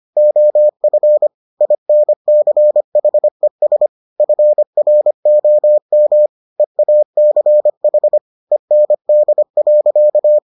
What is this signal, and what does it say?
Morse code, heard as a keyed tone, about 25 wpm, 600 Hz. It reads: OF INCHES FROM EACH END.